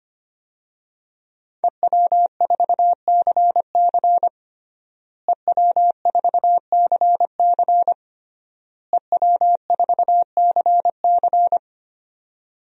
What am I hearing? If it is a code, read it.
Morse code, heard as a keyed tone, about 25 wpm, 700 Hz. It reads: EW4CC EW4CC EW4CC
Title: DXCC-100 Call Signs - 5 Characters - Repeated 3x - 25wpm